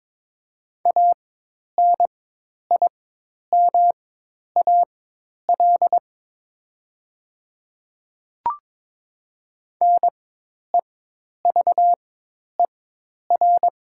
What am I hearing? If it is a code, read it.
Morse code, heard as a keyed tone, about 22 wpm, spaced as for 10 wpm, 700 Hz. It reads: ANIMAL NEVER
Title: 4x Character Spacing - Top 300 Words - 22wpm